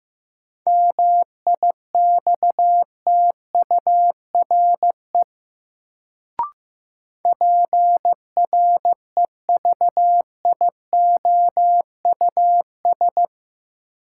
Morse code, heard as a keyed tone, about 15 wpm, 700 Hz. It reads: MIXTURE PREVIOUS
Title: Sentences from Top 1500 Words - New Word Drill Alt - 15wpm